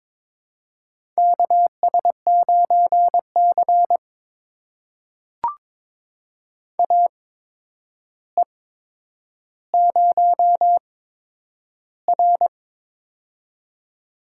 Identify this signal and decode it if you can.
Morse code, heard as a keyed tone, about 22 wpm, spaced as for 9 wpm, 700 Hz. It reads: KS9C AE0R
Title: USA Call Signs - 4 Characters - 8x Character Spacing - 22wpm